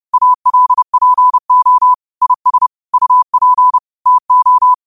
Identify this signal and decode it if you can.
Morse code, heard as a keyed tone, about 30 wpm, 1000 Hz. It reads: ALPO IS UP TO